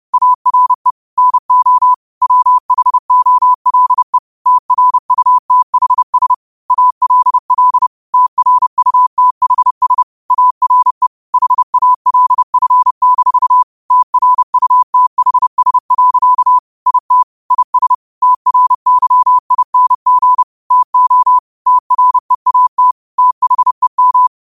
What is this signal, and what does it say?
Morse code, heard as a keyed tone, about 30 wpm, 1000 Hz. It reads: ARE NO WHOLE TRUTHS ALL TRUTHS ARE HALF- TRUTHS. IT IS TRYING TO TREAT THEM
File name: 30wpm